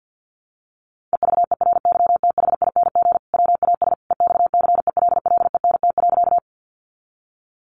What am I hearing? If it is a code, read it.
Morse code, heard as a keyed tone, about 50 wpm, 700 Hz. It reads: EVERYTHING WAS EXCELLENT.
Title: Sentences from Top 1000 Words 50wpm